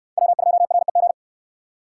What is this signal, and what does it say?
Morse code, heard as a keyed tone, about 61 wpm, 700 Hz. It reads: W2REG